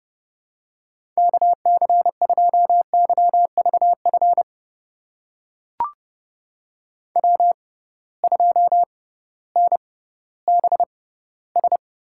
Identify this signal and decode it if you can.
Morse code, heard as a keyed tone, about 32 wpm, spaced as for 17 wpm, 700 Hz. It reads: KC2YVF W2NBS
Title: USA Call Signs - 6x Character Spacing - 30wpm